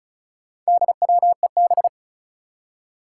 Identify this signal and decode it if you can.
Morse code, heard as a keyed tone, about 36 wpm, 700 Hz. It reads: DWEB
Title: Four Challenging Letter-Number Combinations 35wpm